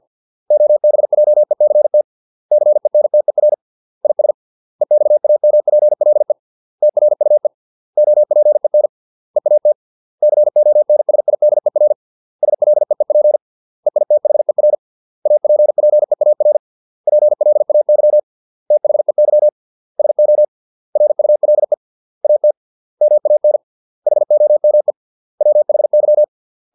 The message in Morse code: OBJECT CENTER IS EXAMPLE TRUE OPEN EAT CONSIDER SLEEP EITHER APPEAR PLAY THEY SO RULE AT MAN HOME WHY

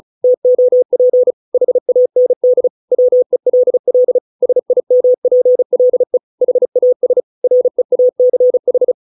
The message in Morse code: TOP HAND WELL SIMPLE HAS REACH

